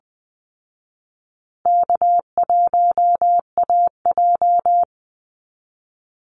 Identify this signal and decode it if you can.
Morse code, heard as a keyed tone, about 20 wpm, 700 Hz. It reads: K1AJ
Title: Call Signs Repeated 3x - 20wpm